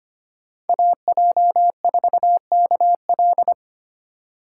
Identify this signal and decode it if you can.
Morse code, heard as a keyed tone, about 25 wpm, 700 Hz. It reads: AJ4KL
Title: USA Call Signs - 5 Characters - 4x Character Spacing - 25wpm